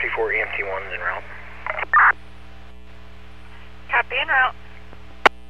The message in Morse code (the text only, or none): none